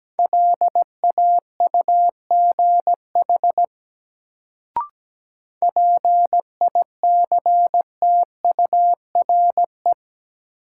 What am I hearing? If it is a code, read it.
Morse code, heard as a keyed tone, about 17 wpm, 700 Hz. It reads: LAUGH PICTURE
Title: Sentences from Top 500 Words - New Word Drill - 17wpm